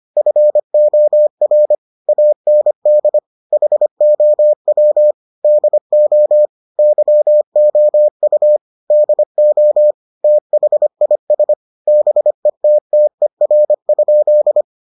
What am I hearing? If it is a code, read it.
Morse code, heard as a keyed tone, about 25 wpm, 600 Hz. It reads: FOR AND HOW DO YOU DO THIS BETTER?